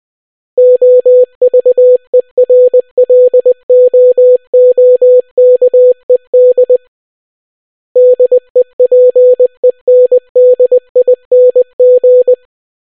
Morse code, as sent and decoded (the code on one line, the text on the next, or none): --- ...- . .-. .-.. --- --- -.- . -.. / -.. .--. -. -.. .. -. --.
OVERLOOKED DEPENDING